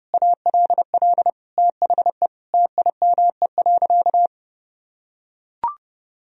Morse code, heard as a keyed tone, about 30 wpm, 700 Hz. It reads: ALL THE TIME.
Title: Sentences from Top 1000 Words - Review Part 4 - 30wpm